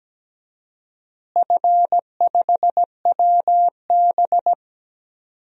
Morse code, heard as a keyed tone, about 17 wpm, 700 Hz. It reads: F5WB
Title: DXCC-100 Call Signs - 4 Characters - Repeated 2x - 17wpm